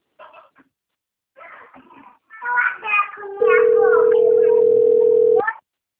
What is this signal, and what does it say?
Ringback tone (North America), 440 + 480 Hz.
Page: Dantas - PE